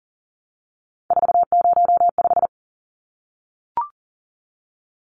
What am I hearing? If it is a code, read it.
Morse code, heard as a keyed tone, about 40 wpm, 700 Hz. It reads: 405
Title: Numbers - 3-Digit 40wpm